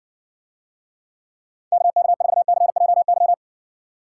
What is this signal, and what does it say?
Morse code, heard as a keyed tone, about 66 wpm, 700 Hz. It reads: KK4CJY